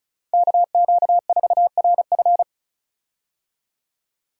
Morse code, heard as a keyed tone, about 36 wpm, 700 Hz. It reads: KQ4RF
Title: USA Call Signs - 5 Characters - 35wpm